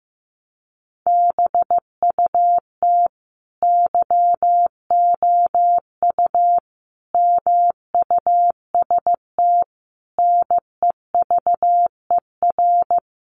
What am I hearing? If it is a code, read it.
Morse code, heard as a keyed tone, about 15 wpm, 700 Hz. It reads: BUT YOU MUST NEVER
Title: Sentences from Top 300 Words 15wpm